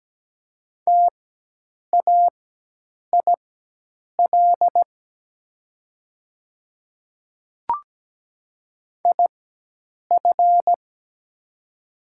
Morse code, heard as a keyed tone, about 17 wpm, spaced as for 8 wpm, 700 Hz. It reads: TAIL IF
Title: 4x Character Spacing - Top 500 Words - 17wpm